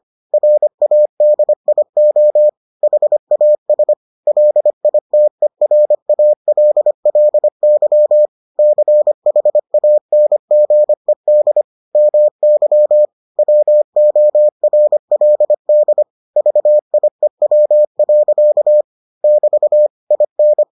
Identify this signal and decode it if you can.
Morse code, heard as a keyed tone, about 25 wpm, 600 Hz. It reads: RADIO HAS LITERALLY CHANGED MY WORLD VIEW. <BT> IN